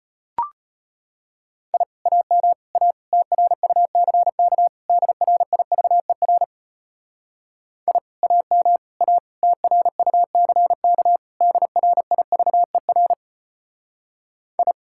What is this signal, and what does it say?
Morse code, heard as a keyed tone, about 31 wpm, 700 Hz. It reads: I AM A TRUCK DRIVER I AM A TRUCK DRIVER I
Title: Speed-Racing Common QSO Phrases - 25wpm